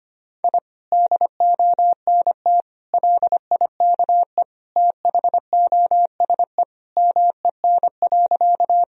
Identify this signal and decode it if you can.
Morse code, heard as a keyed tone, about 25 wpm, 700 Hz. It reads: I DONT LIKE THOSE MEN.
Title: Sentences from Top 1000 Words - Review Part 1 - 25wpm